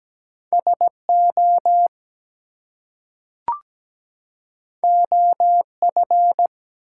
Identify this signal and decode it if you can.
Morse code, heard as a keyed tone, about 17 wpm, 700 Hz. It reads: SO OF
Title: Two Letter Words - 17wpm